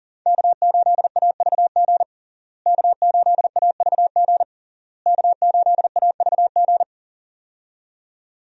Morse code, heard as a keyed tone, about 40 wpm, 700 Hz. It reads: K8AVG K8AVG K8AVG
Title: USA Call Signs - 5 Characters - Repeated 3x - 40wpm